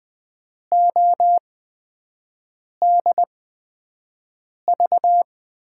Morse code, heard as a keyed tone, about 20 wpm, spaced as for 5 wpm, 700 Hz. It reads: ODV
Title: USA Call Signs - 8x Character Spacing - 20wpm